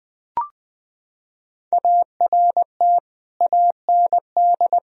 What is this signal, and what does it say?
Morse code, heard as a keyed tone, about 20 wpm, 700 Hz. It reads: ART AND
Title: Sentences from Top 1000 Words - Review Part 3 - 20wpm